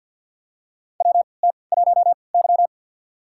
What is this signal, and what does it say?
Morse code, heard as a keyed tone, about 51 wpm, spaced as for 28 wpm, 700 Hz. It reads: WT1Y
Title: USA Call Signs - 3x Character Spacing - 50wpm